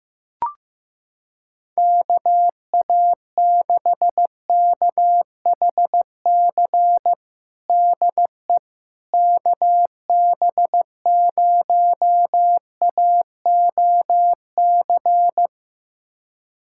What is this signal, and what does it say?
Morse code, heard as a keyed tone, about 15 wpm, 700 Hz. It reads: KA6KHC DE KB0AOC
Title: USA Callsign DE Callsign - 6 Characters - 15wpm